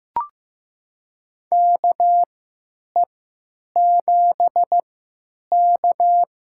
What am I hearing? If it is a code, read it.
Morse code, heard as a keyed tone, about 15 wpm, spaced as for 9 wpm, 700 Hz. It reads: KE7K
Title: USA Call Signs - 3x Character Spacing - 15wpm